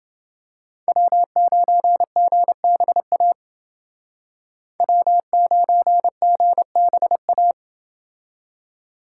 Morse code, heard as a keyed tone, about 29 wpm, 700 Hz. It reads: W9GBA W9GBA